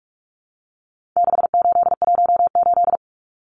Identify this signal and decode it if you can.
Morse code, heard as a keyed tone, about 44 wpm, 700 Hz. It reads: 6818
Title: Numbers - 4-Digit 45wpm